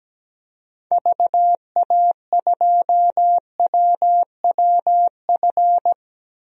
Morse code, heard as a keyed tone, about 17 wpm, 700 Hz. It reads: VA2WWF
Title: DXCC-100 Call Signs - 6 Characters - 17wpm